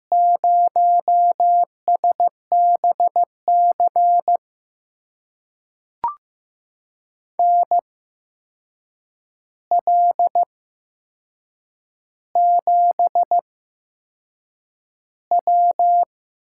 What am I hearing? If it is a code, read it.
Morse code, heard as a keyed tone, about 15 wpm, spaced as for 6 wpm, 700 Hz. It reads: 0SBC NL7W
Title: USA Call Signs - 5 Characters - 8x Character Spacing - 15wpm